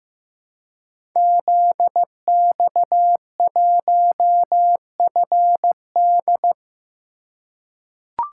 Morse code, heard as a keyed tone, about 15 wpm, 700 Hz. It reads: ZX1FD
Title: Code Groups - 5 Letter:Numbers - 15wpm